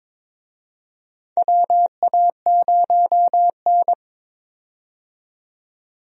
Morse code, heard as a keyed tone, about 22 wpm, 700 Hz. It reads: WA0N
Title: USA Call Signs - 4 Characters - 22wpm